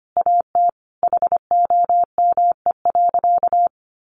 Morse code, heard as a keyed tone, about 25 wpm, 700 Hz. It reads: AT HOME.
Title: Sentences from Top 1400 Words 25wpm